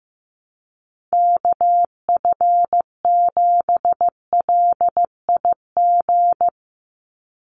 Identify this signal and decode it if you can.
Morse code, heard as a keyed tone, about 15 wpm, 700 Hz. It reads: KF7LIG
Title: USA Call Signs 15wpm